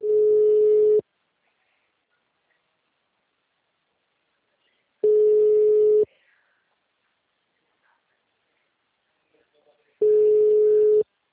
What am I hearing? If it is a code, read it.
Ringback tone (Europe, 425 Hz), with speech.